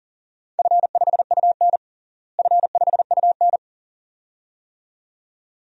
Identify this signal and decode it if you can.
Morse code, heard as a keyed tone, about 40 wpm, 700 Hz. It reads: F5UN F5UN
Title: DXCC-100 Call Signs - 4 Characters - Repeated 2x - 40wpm